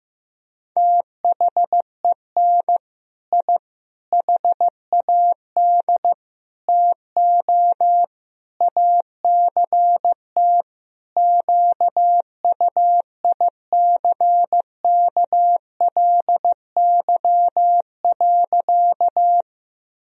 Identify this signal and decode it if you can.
Morse code, heard as a keyed tone, about 15 wpm, 700 Hz. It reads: THEN I HAD TO ACT QUICKLY.
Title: Sentences from Top 1000 Words - Review Part 1 - 15wpm